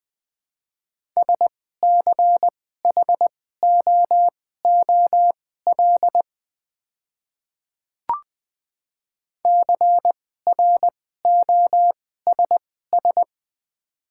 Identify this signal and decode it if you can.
Morse code, heard as a keyed tone, about 20 wpm, spaced as for 14 wpm, 700 Hz. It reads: SCHOOL CROSS
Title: 2x Character Spacing - Top 300 Words - 20wpm